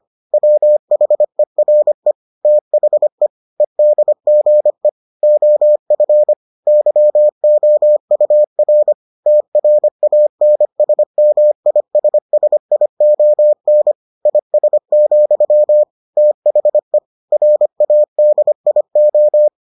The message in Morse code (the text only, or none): WHERE THE EDGE OF YOUR TRANSMISSION IS, THE RADIO